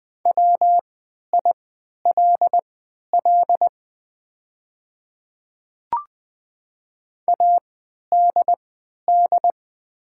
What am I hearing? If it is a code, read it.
Morse code, heard as a keyed tone, about 20 wpm, spaced as for 11 wpm, 700 Hz. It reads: WILL ADD
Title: 3x Character Spacing - Top 200 Words - 20wpm